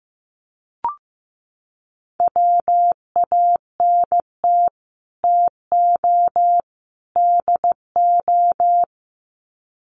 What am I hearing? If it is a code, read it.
Morse code, heard as a keyed tone, about 15 wpm, 700 Hz. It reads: WANT TO DO
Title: Sets of 3 Words 15wpm